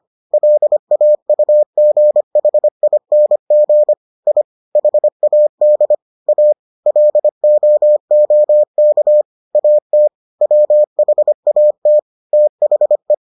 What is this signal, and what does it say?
Morse code, heard as a keyed tone, about 25 wpm, 600 Hz. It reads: LAUGHING I HAD A LOOK AT WHAT THE